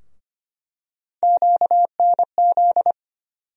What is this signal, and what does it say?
Morse code, heard as a keyed tone, about 25 wpm, 700 Hz. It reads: QNZ
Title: USA Call Sign Postfixes - 3 Letters 25wpm